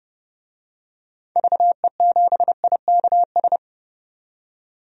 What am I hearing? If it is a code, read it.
Morse code, heard as a keyed tone, about 30 wpm, 700 Hz. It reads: VE7IKS
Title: DXCC-100 Call Signs - 6 Characters - 30wpm